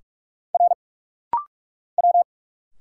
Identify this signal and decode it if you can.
Morse code, heard as a keyed tone, about 46 wpm, 700 Hz. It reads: R W
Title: Single Letters - Rapid-Fire 45wpm